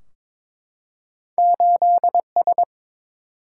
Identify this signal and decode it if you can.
Morse code, heard as a keyed tone, about 22 wpm, 700 Hz. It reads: 8 S
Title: Single-Letter Single-Number 22wpm